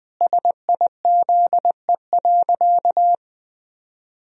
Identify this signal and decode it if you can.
Morse code, heard as a keyed tone, about 20 wpm, 700 Hz. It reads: SIZE.
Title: Sentences from Top 400 Words and Abbreviations 20wpm